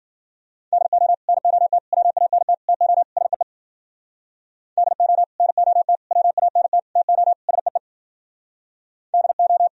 Morse code, heard as a keyed tone, about 52 wpm, 700 Hz. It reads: DO NOT WANT TO SEE DO NOT WANT TO SEE DO